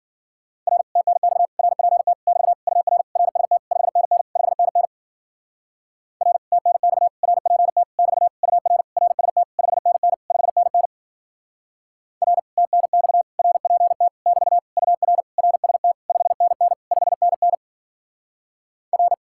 Morse code, heard as a keyed tone, about 48 wpm, 700 Hz. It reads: R TNX RPT <BT> UR RST 5NN 5NN R TNX RPT <BT> UR RST 5NN 5NN R TNX RPT <BT> UR RST 5NN 5NN R
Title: Speed-Racing Common QSO Phrases - 40wpm